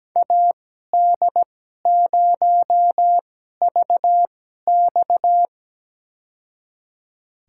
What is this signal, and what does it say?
Morse code, heard as a keyed tone, about 17 wpm, spaced as for 12 wpm, 700 Hz. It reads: AD0VX
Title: USA Call Signs - 5 Characters - 2x Character Spacing - 17wpm